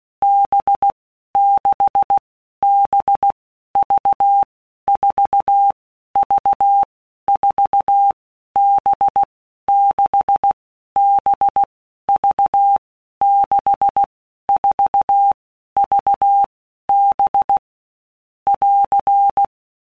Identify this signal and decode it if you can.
Morse code, heard as a keyed tone, about 16 wpm, 800 Hz. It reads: B 6 B V 4 V 4 B 6 B V 6 4 V B <AR>
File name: Basic Letter Groups By WPM's / Groups@15 / VB46@15